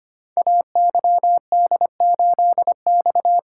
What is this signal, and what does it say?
Morse code, heard as a keyed tone, about 25 wpm, 700 Hz. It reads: AYD8X
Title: Code Groups - 5 Letter:Numbers - 25wpm